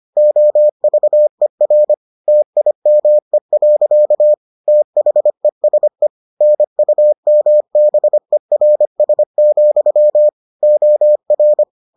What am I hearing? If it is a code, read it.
Morse code, heard as a keyed tone, about 25 wpm, 600 Hz. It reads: OVER TIME. THESE NUMBERS, OR